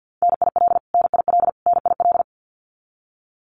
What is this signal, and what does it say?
Morse code, heard as a keyed tone, about 50 wpm, 700 Hz. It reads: NIL NIL NIL
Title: Abbreviations Repeated 3x - 50wpm